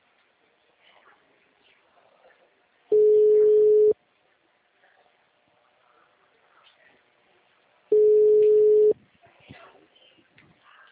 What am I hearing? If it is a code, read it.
Ringback tone (Europe, 425 Hz).